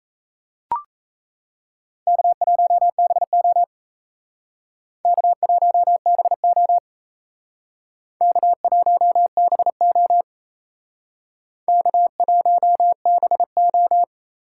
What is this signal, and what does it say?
Morse code, heard as a keyed tone, about 36 wpm, 700 Hz. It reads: K1BO K1BO K1BO K1BO